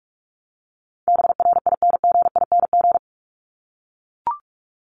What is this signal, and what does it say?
Morse code, heard as a keyed tone, about 44 wpm, 700 Hz. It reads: BRINGING
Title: ING Suffix - Top 100 Words with Base Words 45wpm